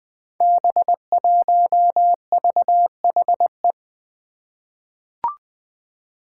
Morse code, heard as a keyed tone, about 20 wpm, 700 Hz. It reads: B1VHE
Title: Five Challenging Letter-Number Combinations 20wpm